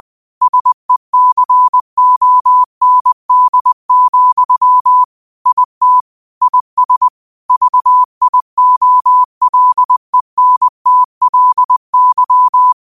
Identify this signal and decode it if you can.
Morse code, heard as a keyed tone, about 20 wpm, 1000 Hz. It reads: SECOND, IT IS VIOLENTLY
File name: qotdfiles / 20wpm